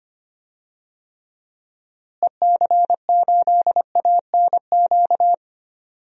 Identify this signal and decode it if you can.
Morse code, heard as a keyed tone, about 25 wpm, 700 Hz. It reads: EC8ANQ
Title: DXCC-100 Call Signs - 6 Characters - Repeated 2x - 25wpm